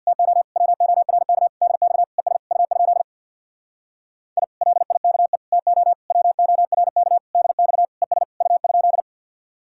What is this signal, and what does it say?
Morse code, heard as a keyed tone, about 66 wpm, 700 Hz. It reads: TO WORK DX ES U? I LIKE TO WORK DX ES U?